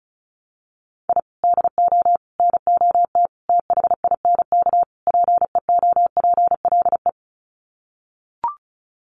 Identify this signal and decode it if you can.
Morse code, heard as a keyed tone, about 34 wpm, 700 Hz. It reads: I DO NOT THINK PEOPLE
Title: Sets of 5 Words Encore2 35wpm